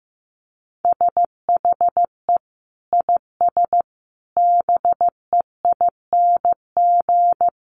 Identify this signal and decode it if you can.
Morse code, heard as a keyed tone, about 15 wpm, 700 Hz. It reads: SHE IS BEING
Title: Sentences from Top 600 Words 15wpm